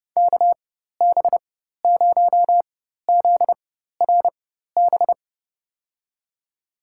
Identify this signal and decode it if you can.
Morse code, heard as a keyed tone, about 32 wpm, spaced as for 14 wpm, 700 Hz. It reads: KB0ZRB
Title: USA Call Signs - 6 Characters - 4x Character Spacing - 30wpm